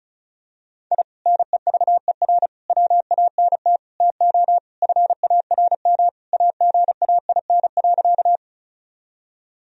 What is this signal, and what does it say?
Morse code, heard as a keyed tone, about 36 wpm, 700 Hz. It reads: I NEVER WANT TO FARM AGAIN.
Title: Sentences from Top 1000 Words - Review Part 2 - 35wpm